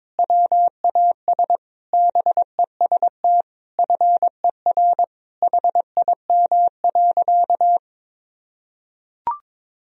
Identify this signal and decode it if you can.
Morse code, heard as a keyed tone, about 22 wpm, 700 Hz. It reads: WAS BEST FER HIM.
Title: Sentences from Top 500 Words and Abbreviations 22wpm